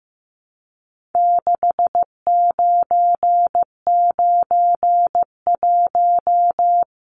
Morse code, heard as a keyed tone, about 15 wpm, 700 Hz. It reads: 6991
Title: Numbers - 4-Digit 15wpm